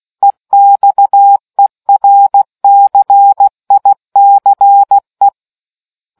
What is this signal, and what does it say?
Morse code, heard as a keyed tone, about 16 wpm, 800 Hz. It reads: EXERCICE